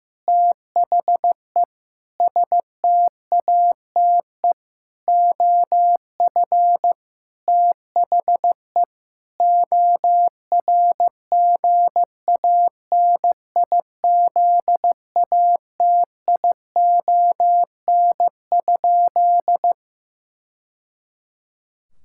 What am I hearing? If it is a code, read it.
Morse code, heard as a keyed tone, about 15 wpm, 700 Hz. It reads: THE STATE OF THE ORGANIZATION?
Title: Sentences from Top 1000 Words - Review Part 5 - 15wpm